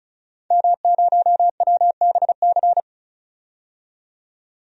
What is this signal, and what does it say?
Morse code, heard as a keyed tone, about 36 wpm, 700 Hz. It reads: M0WBC
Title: DXCC-100 Call Signs - 5 Characters - 35wpm